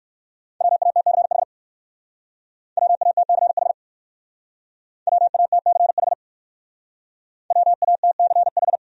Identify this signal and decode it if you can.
Morse code, heard as a keyed tone, about 56 wpm, 700 Hz. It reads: WATCH WATCH WATCH WATCH
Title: Speed-Racing Top 500 Words 45wpm